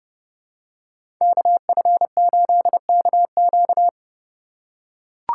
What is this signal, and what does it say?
Morse code, heard as a keyed tone, about 30 wpm, 700 Hz. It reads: KF8KQ
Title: USA Call Signs - 5 Characters - 3x Character Spacing - 30wpm